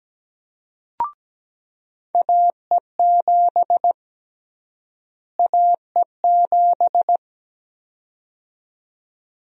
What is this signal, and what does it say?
Morse code, heard as a keyed tone, about 17 wpm, 700 Hz. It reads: AE7 AE7
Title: USA Call Sign Prefixes - Repeated 2x - 17wpm